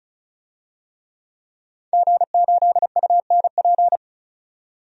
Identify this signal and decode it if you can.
Morse code, heard as a keyed tone, about 34 wpm, 700 Hz. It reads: G8UNP
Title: DXCC-100 Call Signs - 5 Characters - 35wpm